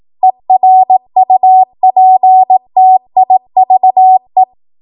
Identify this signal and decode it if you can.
Morse code, heard as a keyed tone, about 18 wpm, 750 Hz. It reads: ERUPTIVE
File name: Morse / 18WPM